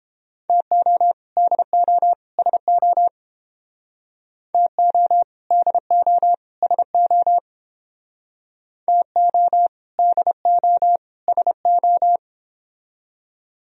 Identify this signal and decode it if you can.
Morse code, heard as a keyed tone, about 28 wpm, 700 Hz. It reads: TO DO SO TO DO SO TO DO SO